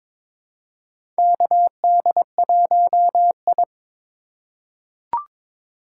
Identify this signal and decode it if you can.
Morse code, heard as a keyed tone, about 22 wpm, 700 Hz. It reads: KD1I
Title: USA Call Signs - 4 Characters - 3x Character Spacing - 22wpm